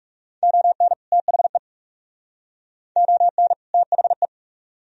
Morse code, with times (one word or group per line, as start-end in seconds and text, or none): 0.4-0.9 ON
1.1-1.6 THE
3.0-3.5 ON
3.7-4.3 THE